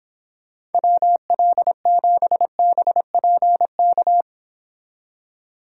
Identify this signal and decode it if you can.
Morse code, heard as a keyed tone, about 29 wpm, 700 Hz. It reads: WL7BPK